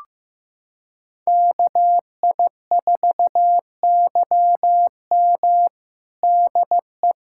Morse code, heard as a keyed tone, about 15 wpm, 700 Hz. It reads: KI4YM DE
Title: USA Callsign DE Callsign - 5 Characters - 15wpm